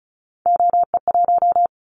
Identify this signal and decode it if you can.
Morse code, heard as a keyed tone, about 34 wpm, 700 Hz. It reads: OE1
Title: Callsign Prefixes for Top 100 DXCC - 35wpm